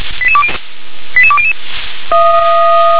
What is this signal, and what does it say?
Five-tone selective call (ZVEI), heard as quick runs of five tones, tones of about 70 ms. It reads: 88200